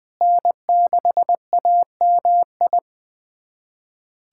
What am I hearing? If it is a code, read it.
Morse code, heard as a keyed tone, about 20 wpm, 700 Hz. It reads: N6AMI
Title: USA Callsign DE Callsign 20wpm